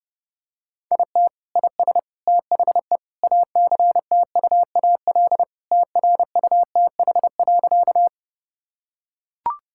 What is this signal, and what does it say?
Morse code, heard as a keyed tone, about 30 wpm, 700 Hz. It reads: IT IS THE ACTUAL TRUTH.
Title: Sentences from Top 1000 Words - Review Part 5 - 30wpm